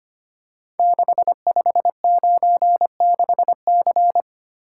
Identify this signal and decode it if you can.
Morse code, heard as a keyed tone, about 25 wpm, 700 Hz. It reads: 6596C
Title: Code Groups - 5 Letter:Numbers - 25wpm